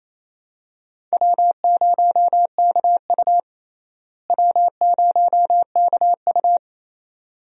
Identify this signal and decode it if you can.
Morse code, heard as a keyed tone, about 28 wpm, 700 Hz. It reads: W0KU W0KU
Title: USA Call Signs - 4 Characters - Repeated 2x - 28wpm